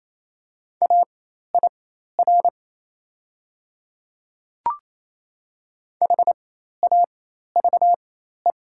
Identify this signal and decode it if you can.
Morse code, heard as a keyed tone, about 28 wpm, spaced as for 13 wpm, 700 Hz. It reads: AIR HAVE
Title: 4x Character Spacing - Top 300 Words - 28wpm